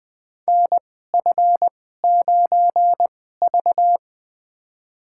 Morse code, heard as a keyed tone, about 20 wpm, spaced as for 14 wpm, 700 Hz. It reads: NF9V
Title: USA Call Signs - 4 Characters - 2x Character Spacing - 20wpm